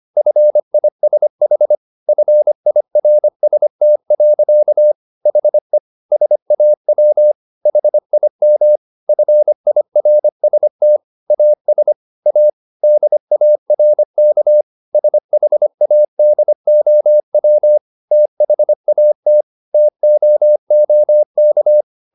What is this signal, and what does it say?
Morse code, heard as a keyed tone, about 25 wpm, 600 Hz. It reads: FISH FIRST. HE SAW HIM FIRST AS A DARK SHADOW THAT TOOK